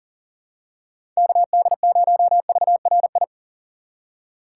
Morse code, heard as a keyed tone, about 37 wpm, 700 Hz. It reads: KD0VRI
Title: USA Call Signs - 6 Characters - 6x Character Spacing - 40wpm